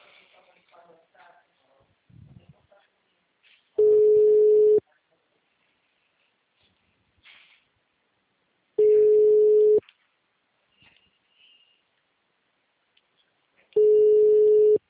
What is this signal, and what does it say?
Ringback tone (Europe, 425 Hz), with speech.